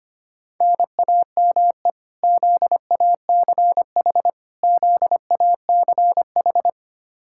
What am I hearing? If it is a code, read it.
Morse code, heard as a keyed tone, about 25 wpm, 700 Hz. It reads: NAME ZACH ZACH
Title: Practice QSO - Exchange 1 - Beginning - 25wpm